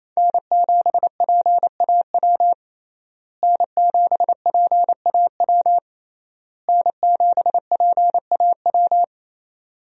Morse code, heard as a keyed tone, about 28 wpm, 700 Hz. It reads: N7PAW N7PAW N7PAW
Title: USA Call Signs - 5 Characters - Repeated 3x - 28wpm